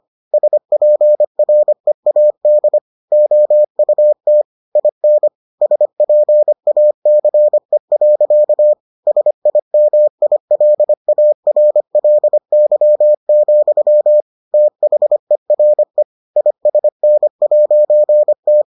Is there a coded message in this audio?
Yes: Morse code from 0.3 to 18.6 s, heard as a keyed tone, about 25 wpm, 600 Hz. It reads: SPREAD OUT IN SPACE. SIMILARLY, THERE ISN'T